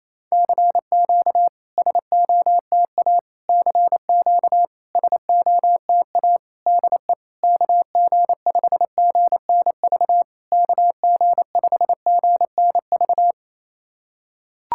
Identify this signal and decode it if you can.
Morse code, heard as a keyed tone, about 29 wpm, 700 Hz. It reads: CQ SOTA CQ SOTA DE KG5GNV KG5GNV
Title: SOTA QSO Practice 28wpm